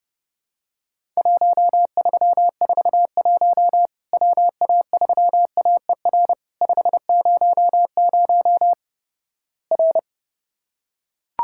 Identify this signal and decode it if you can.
Morse code, heard as a keyed tone, about 30 wpm, 700 Hz. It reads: 1341 WA3AER 500 R
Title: Worked All Europe DX Contest - QTC Exchange - 30wpm